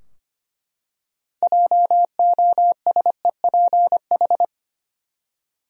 Morse code, heard as a keyed tone, about 25 wpm, 700 Hz. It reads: JOSEPH
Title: USA Common Names - 25wpm